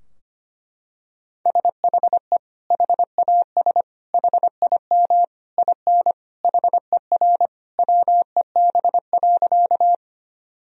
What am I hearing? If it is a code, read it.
Morse code, heard as a keyed tone, about 25 wpm, 700 Hz. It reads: SHE HAS HIM IN HER WEB.
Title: Sentences from Top 1000 Words - Review Part 4 - 25wpm